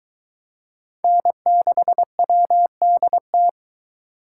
Morse code, heard as a keyed tone, about 22 wpm, 700 Hz. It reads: N6WDT